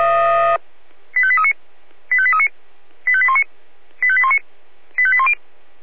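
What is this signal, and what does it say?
Five-tone selective call (ZVEI), heard as quick runs of five tones, tones of about 70 ms. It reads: 85839 85839 85819 85819 85810
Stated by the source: ZVEI five-tone selective call received on SDR.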